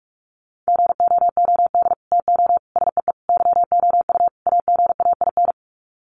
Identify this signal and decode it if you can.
Morse code, heard as a keyed tone, about 44 wpm, 700 Hz. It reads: GOOD TO SEE YOU AGAIN
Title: Sets of 5 Words Encore2 45wpm